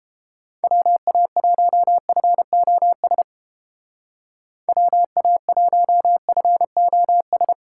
Morse code, heard as a keyed tone, about 30 wpm, 700 Hz. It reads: WA1FOS WA1FOS